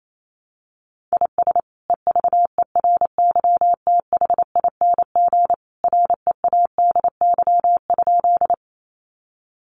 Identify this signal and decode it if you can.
Morse code, heard as a keyed tone, about 28 wpm, 700 Hz. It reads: IS EVERYTHING READY?
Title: Sentences from Top 700 Words 28wpm